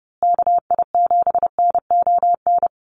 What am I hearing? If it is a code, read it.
Morse code, heard as a keyed tone, about 30 wpm, 700 Hz. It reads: KI7NON
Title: USA Call Signs Repeated 3x 30wpm